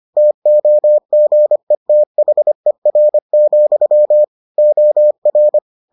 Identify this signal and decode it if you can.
Morse code, heard as a keyed tone, about 25 wpm, 600 Hz. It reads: TOGETHER, OR